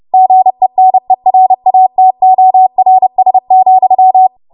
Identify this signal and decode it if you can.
Morse code, heard as a keyed tone, about 30 wpm, 750 Hz. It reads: GENERATORS,
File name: Morse / 30WPM